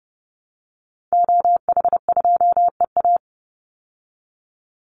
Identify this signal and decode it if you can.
Morse code, heard as a keyed tone, about 30 wpm, 700 Hz. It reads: OH2EA
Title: Call Signs Repeated 3x - 30wpm